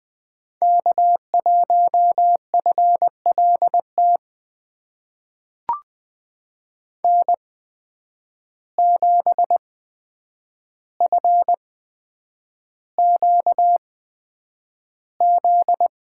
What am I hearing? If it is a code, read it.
Morse code, heard as a keyed tone, about 20 wpm, spaced as for 9 wpm, 700 Hz. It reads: K1FLT N7FQZ
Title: USA Call Signs - 5 Characters - 8x Character Spacing - 20wpm